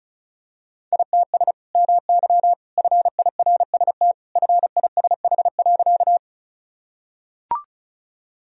Morse code, heard as a keyed tone, about 34 wpm, 700 Hz. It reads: ITS MY FIRST FISH.
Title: Sentences from Top 1000 Words - Review Part 1 - 35wpm